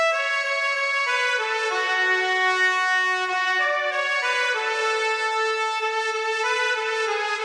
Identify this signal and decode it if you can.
Dial-up modem handshake.